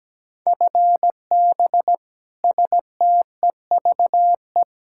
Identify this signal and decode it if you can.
Morse code, heard as a keyed tone, about 17 wpm, 700 Hz. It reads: FB STEVE
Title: Practice QSO - Exchange 3 - More Info - 17wpm